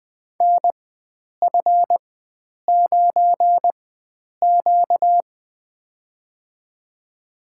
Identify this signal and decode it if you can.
Morse code, heard as a keyed tone, about 21 wpm, spaced as for 9 wpm, 700 Hz. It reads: NF9Q
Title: USA Call Signs - 4x Character Spacing - 20wpm